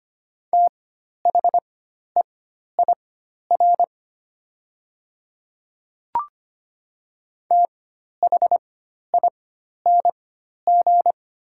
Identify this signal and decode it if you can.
Morse code, heard as a keyed tone, about 25 wpm, spaced as for 12 wpm, 700 Hz. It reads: THEIR THING
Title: 4x Character Spacing - Top 100 Words - 25wpm